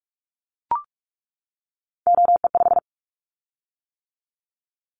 Morse code, heard as a keyed tone, about 44 wpm, 700 Hz. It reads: OE5
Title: Callsign Prefixes for Top 100 DXCC - 45wpm